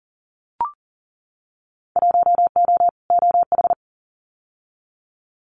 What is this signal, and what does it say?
Morse code, heard as a keyed tone, about 40 wpm, 700 Hz. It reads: 1O OH
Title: Winter Field Day Exchanges 40wpm